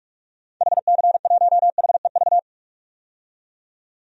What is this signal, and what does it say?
Morse code, heard as a keyed tone, about 43 wpm, 700 Hz. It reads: HC1HEV
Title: DXCC-100 Call Signs - 6 Characters - 45wpm